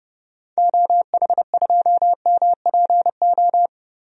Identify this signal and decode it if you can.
Morse code, heard as a keyed tone, about 30 wpm, 700 Hz. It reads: OH2MPO
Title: DXCC-100 Call Signs - 6 Characters - Repeated 3x - 30wpm